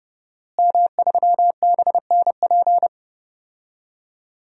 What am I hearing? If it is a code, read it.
Morse code, heard as a keyed tone, about 30 wpm, 700 Hz. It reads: M3BNP
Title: DXCC-100 Call Signs - 5 Characters - Repeated 3x - 30wpm